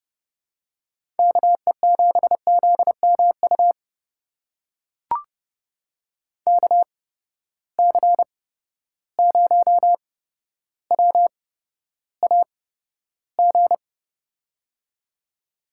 Morse code, heard as a keyed tone, about 30 wpm, spaced as for 13 wpm, 700 Hz. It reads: KE7ZMU KC0WAG
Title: USA Call Signs - 8x Character Spacing - 30wpm